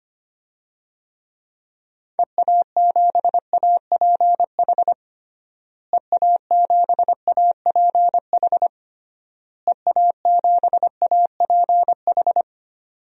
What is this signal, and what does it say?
Morse code, heard as a keyed tone, about 25 wpm, 700 Hz. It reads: EA7APH EA7APH EA7APH
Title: DXCC-100 Call Signs - 6 Characters - Repeated 3x - 25wpm